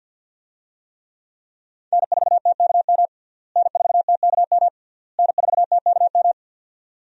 Morse code, heard as a keyed tone, about 52 wpm, 700 Hz. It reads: N4TKM N4TKM N4TKM
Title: USA Call Signs - 5 Characters - Repeated 3x - 50wpm